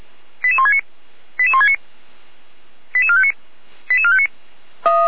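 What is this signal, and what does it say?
Five-tone selective call (ZVEI), heard as quick runs of five tones, tones of about 70 ms. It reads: 88169 88169 88459 88459
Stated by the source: ZVEI five-tone selective call received via SDR, address 88303.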